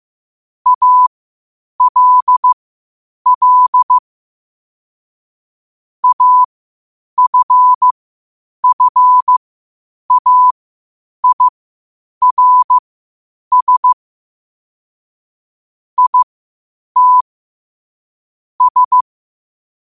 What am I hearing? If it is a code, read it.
Morse code, heard as a keyed tone, about 15 wpm, spaced as for 8 wpm, 1000 Hz. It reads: ALL AFFAIRS IT S
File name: qotdfiles / 8wpm